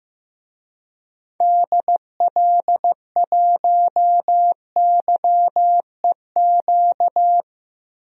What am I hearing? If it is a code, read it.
Morse code, heard as a keyed tone, about 15 wpm, 700 Hz. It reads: DL1YEQ
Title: DXCC-100 Call Signs - 6 Characters - 15wpm